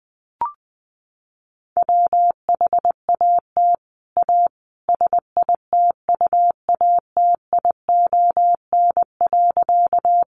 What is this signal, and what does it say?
Morse code, heard as a keyed tone, about 20 wpm, 700 Hz. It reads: WHAT A SITUATION.
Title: Sentences from Top 1200 Words 20wpm